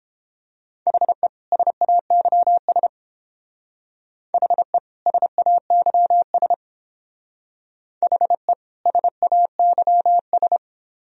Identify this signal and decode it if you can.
Morse code, heard as a keyed tone, about 29 wpm, 700 Hz. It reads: HE SAYS HE SAYS HE SAYS